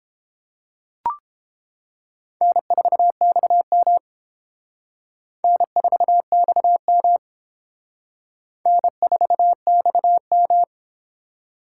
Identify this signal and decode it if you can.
Morse code, heard as a keyed tone, about 28 wpm, 700 Hz. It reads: N4XM N4XM N4XM
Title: Speed Racing - USA 1x2 Call Signs 22wpm